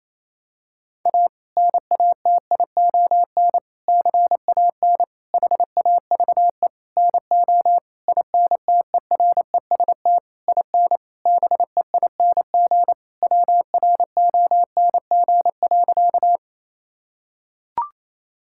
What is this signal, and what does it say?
Morse code, heard as a keyed tone, about 28 wpm, 700 Hz. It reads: A NATION CAN HAVE NO INTEREST IN BEING WRONG.
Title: Sentences from Top 1000 Words - Review Part 6 - 28wpm